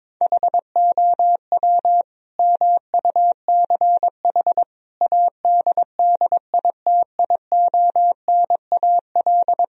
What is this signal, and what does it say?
Morse code, heard as a keyed tone, about 22 wpm, 700 Hz. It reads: HOW MUCH ADDITIONAL
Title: Sentences from Top 1000 Words - Review Part 4 - 22wpm